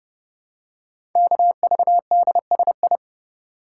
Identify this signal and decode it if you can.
Morse code, heard as a keyed tone, about 29 wpm, 700 Hz. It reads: KVDSI
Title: Code Groups - 5 Letters - 30wpm